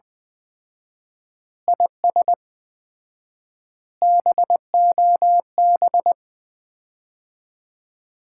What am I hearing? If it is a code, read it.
Morse code, heard as a keyed tone, about 20 wpm, 700 Hz. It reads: IS BOB
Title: Common QSO Phrases - 4x Word-Spacing - 20wpm